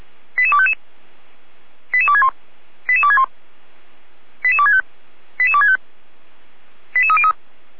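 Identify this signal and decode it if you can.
Five-tone selective call (ZVEI), heard as quick runs of five tones, tones of about 70 ms. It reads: 88255 88261 88261 88265 88265 88303
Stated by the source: ZVEI five-tone selective call received with an SDR, Feuerwehr Orlamuende fire brigade.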